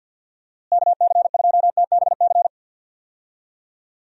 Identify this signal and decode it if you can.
Morse code, heard as a keyed tone, about 48 wpm, 700 Hz. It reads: KC2TBC
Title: USA Call Signs - 6 Characters - Repeated 3x - 50wpm